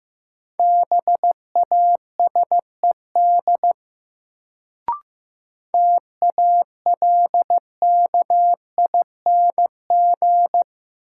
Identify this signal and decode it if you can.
Morse code, heard as a keyed tone, about 15 wpm, 700 Hz. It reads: BASED TALKING
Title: Sets of 2 Words - New Word Drill - 15wpm